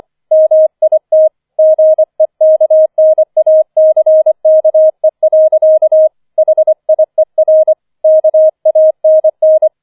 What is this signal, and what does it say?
Morse code, heard as a keyed tone, about 24 wpm, 630 Hz. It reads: MIT GEKNACKE. HIER KANN